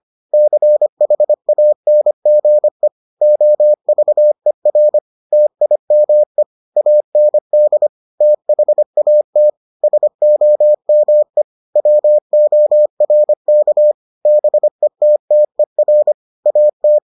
Morse code, heard as a keyed tone, about 25 wpm, 600 Hz. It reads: CHANGE OVER TIME AND THAT SOME WORK BETTER AT